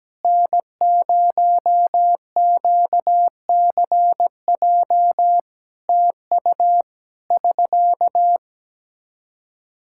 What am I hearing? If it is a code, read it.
Morse code, heard as a keyed tone, about 17 wpm, 700 Hz. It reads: N0QCJ TU <SK>
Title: Practice QSO - Exchange 4 - Ending - 17wpm